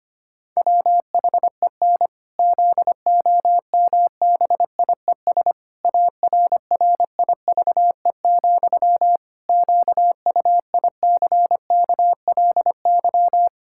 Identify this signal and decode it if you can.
Morse code, heard as a keyed tone, about 25 wpm, 700 Hz. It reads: WHEN ZOMBIES ARRIVE, QUICKLY
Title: Pangram Challenge v2 - 25wpm